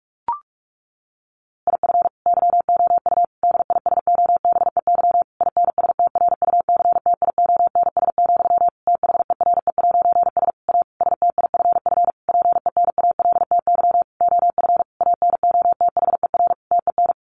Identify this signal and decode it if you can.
Morse code, heard as a keyed tone, about 44 wpm, 700 Hz. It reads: IF YOU DISOBEY INSTRUCTIONS, THERE'S A STIFF PENALTY OF ANOTHER TEN
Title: A Sound Of Thunder 45wpm